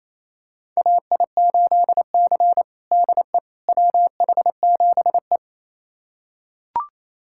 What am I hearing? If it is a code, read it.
Morse code, heard as a keyed tone, about 28 wpm, 700 Hz. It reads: AI8C DE WH7E
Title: USA Callsign DE Callsign - 4 Characters - 28wpm